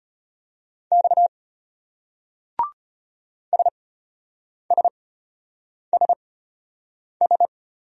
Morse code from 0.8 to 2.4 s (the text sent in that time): X